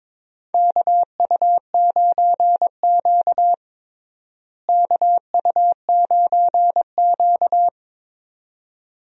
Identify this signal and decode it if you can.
Morse code, heard as a keyed tone, about 22 wpm, 700 Hz. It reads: KU9Q KU9Q
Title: USA Call Signs - 4 Characters - Repeated 2x - 22wpm